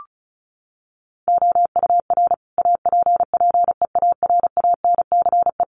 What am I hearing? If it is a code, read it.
Morse code, heard as a keyed tone, about 34 wpm, 700 Hz. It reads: OUR APPEARANCE
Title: Sentences from Top 1100 Words 35wpm